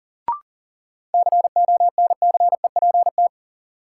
Morse code, heard as a keyed tone, about 37 wpm, 700 Hz. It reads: CONCEPT
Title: Sentences from Top 1100 Words - New Word Drill - 40wpm